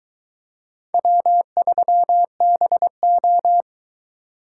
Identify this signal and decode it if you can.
Morse code, heard as a keyed tone, about 23 wpm, 700 Hz. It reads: W3BO